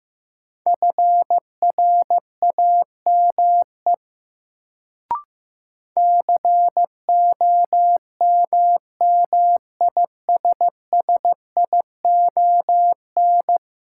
Morse code, heard as a keyed tone, about 15 wpm, 700 Hz. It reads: FRAME COMMISSION
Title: Sentences from Top 1400 Words - New Word Drill Alt - 15wpm